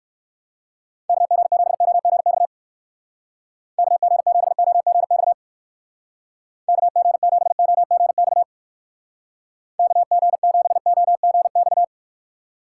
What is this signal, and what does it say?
Morse code, heard as a keyed tone, about 54 wpm, 700 Hz. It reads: KG7OGX KG7OGX KG7OGX KG7OGX
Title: Speed Racing - USA 2x3 Call Signs 45wpm